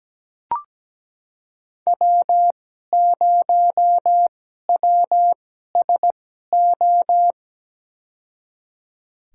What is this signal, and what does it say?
Morse code, heard as a keyed tone, about 17 wpm, spaced as for 12 wpm, 700 Hz. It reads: W0WSO
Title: USA Call Signs - 5 Characters - 2x Character Spacing - 17wpm